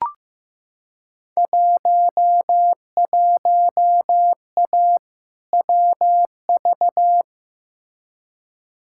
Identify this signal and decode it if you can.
Morse code, heard as a keyed tone, about 15 wpm, 700 Hz. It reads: 11A WV
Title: Field Day Exchanges Encore - 15wpm